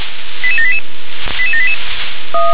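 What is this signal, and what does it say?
Five-tone selective call (ZVEI), heard as quick runs of five tones, tones of about 70 ms. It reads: 88788 88788
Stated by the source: ZVEI five-tone call received by SDR.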